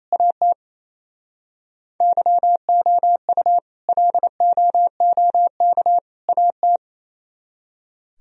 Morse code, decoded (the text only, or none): AT YOU LOOK AT